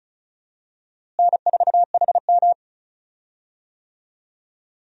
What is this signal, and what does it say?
Morse code, heard as a keyed tone, about 34 wpm, 700 Hz. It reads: N4HM
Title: USA Call Signs - 4 Characters - Repeated 3x - 35wpm